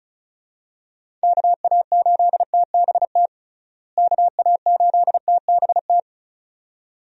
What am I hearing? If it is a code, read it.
Morse code, heard as a keyed tone, about 36 wpm, 700 Hz. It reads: KA8TBT KA8TBT
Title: USA Call Signs - 6 Characters - Repeated 2x - 35wpm